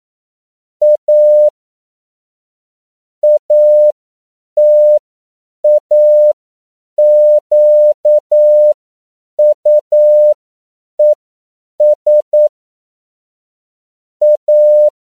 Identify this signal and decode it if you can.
Morse code, heard as a keyed tone, about 9 wpm, spaced as for 7 wpm, 600 Hz. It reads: A ATAQUES A
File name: CW / 10PPM